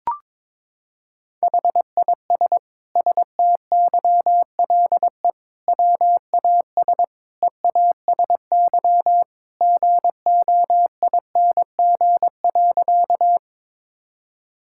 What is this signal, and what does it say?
Morse code, heard as a keyed tone, about 21 wpm, 700 Hz. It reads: HIS STYLE WAS EASY GOING.
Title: Sentences from Top 1000 Words - Review Part 5 - 22wpm